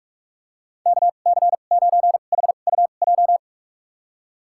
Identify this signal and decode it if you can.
Morse code, heard as a keyed tone, about 46 wpm, spaced as for 32 wpm, 700 Hz. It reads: KC9HUJ
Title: USA Call Signs - 2x Character Spacing - 45wpm